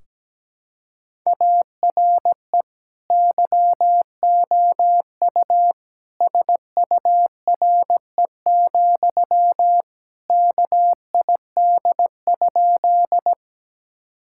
Morse code, heard as a keyed tone, about 17 wpm, 700 Hz. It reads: ARE YOU SURE, KID?
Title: Sentences from Top 1600 Words 17wpm